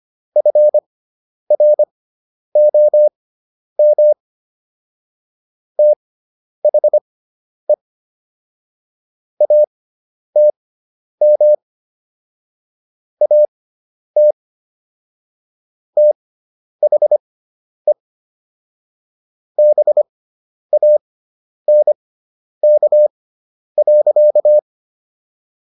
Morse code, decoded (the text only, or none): FROM THE ATM AT THE BANK.